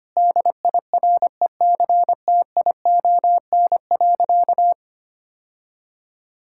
Morse code, heard as a keyed tone, about 25 wpm, 700 Hz. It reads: DIRECTION.
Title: Sentences from Top 1000 Words - Review Part 5 - 25wpm